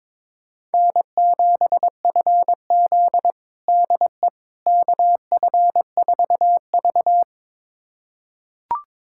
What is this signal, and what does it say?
Morse code, heard as a keyed tone, about 22 wpm, 700 Hz. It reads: N7FZ DE KF4V
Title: USA Callsign DE Callsign - 4 Characters - 22wpm